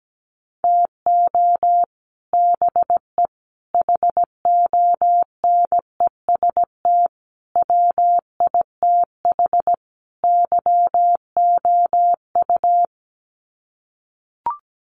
Morse code, heard as a keyed tone, about 17 wpm, 700 Hz. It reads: TO BE HONEST WITH YOU
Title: Sets of 5 Words Encore 17wpm